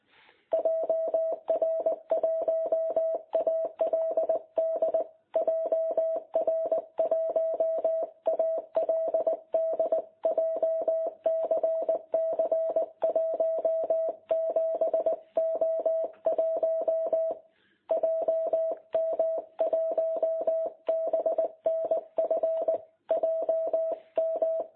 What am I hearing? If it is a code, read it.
Morse code, heard as a keyed tone, about 19 wpm, 660 Hz. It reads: JR1ALD JR1ALD JCC 17O1 JM1BNF JM
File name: jh8cbh / 2021cw